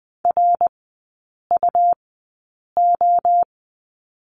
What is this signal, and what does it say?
Morse code, heard as a keyed tone, about 20 wpm, spaced as for 8 wpm, 700 Hz. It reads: RUO
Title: 121 Lesson 20e - Number 3 - Three Characters 20wpm